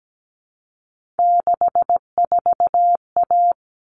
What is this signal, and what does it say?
Morse code, heard as a keyed tone, about 17 wpm, 700 Hz. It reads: 64A
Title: All Cut Numbers - 3-Digit 17wpm